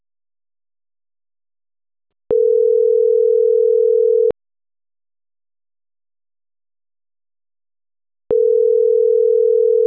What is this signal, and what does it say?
Ringback tone (North America), 440 + 480 Hz.